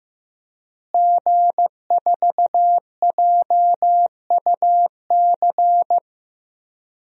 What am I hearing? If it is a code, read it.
Morse code, heard as a keyed tone, about 15 wpm, 700 Hz. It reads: G4JUC
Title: DXCC-100 Call Signs - 5 Characters - Repeated 3x - 15wpm